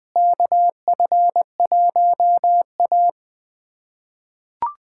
Morse code, heard as a keyed tone, about 21 wpm, 700 Hz. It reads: KF1A
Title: USA Callsign DE Callsign - 4 Characters - 20wpm